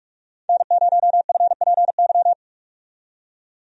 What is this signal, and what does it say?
Morse code, heard as a keyed tone, about 47 wpm, 700 Hz. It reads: N0FPY
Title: USA Call Signs - 5 Characters - 45wpm